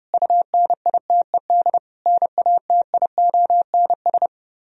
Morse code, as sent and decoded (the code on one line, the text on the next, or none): ..- -. .. - . -.. / -. .- - .. --- -. ...
UNITED NATIONS